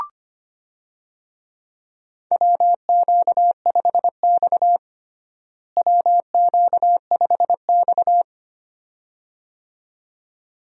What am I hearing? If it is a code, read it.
Morse code, heard as a keyed tone, about 25 wpm, 700 Hz. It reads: WQ5X WQ5X
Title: USA Call Signs - 4 Characters - Repeated 2x - 25wpm